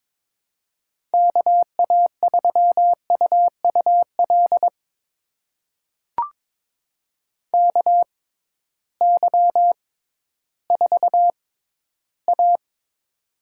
Morse code, heard as a keyed tone, about 23 wpm, spaced as for 13 wpm, 700 Hz. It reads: KA3UUL KY4A
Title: USA Call Signs - 6x Character Spacing - 22wpm